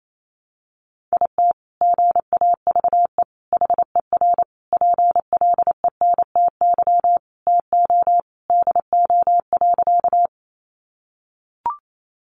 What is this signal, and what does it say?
Morse code, heard as a keyed tone, about 28 wpm, 700 Hz. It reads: IT GAVE HER PLENTY TO DO.
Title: Sentences from Top 1300 Words 28wpm